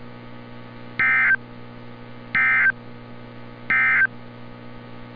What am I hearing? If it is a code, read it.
Emergency Alert System (SAME) tones, heard as harsh buzzing bursts. It is heard from 0.9 to 4.2 s.
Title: NOAA Weather Radio - Altoona, PA